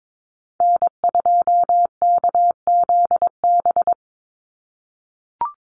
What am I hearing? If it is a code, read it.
Morse code, heard as a keyed tone, about 22 wpm, 700 Hz. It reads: N2KZB
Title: USA Call Signs 22wpm